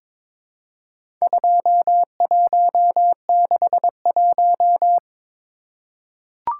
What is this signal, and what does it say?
Morse code, heard as a keyed tone, about 22 wpm, 700 Hz. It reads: 2161
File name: practice / 22wpm / Four Letter-Number Combinations - 22wpm